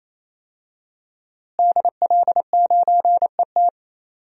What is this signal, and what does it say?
Morse code, heard as a keyed tone, about 27 wpm, 700 Hz. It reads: DL9ET
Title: DXCC-100 Call Signs - 5 Characters - Repeated 3x - 28wpm